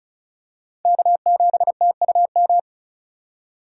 Morse code, heard as a keyed tone, about 36 wpm, 700 Hz. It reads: K7TUM
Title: USA Call Signs - 5 Characters - 3x Character Spacing - 35wpm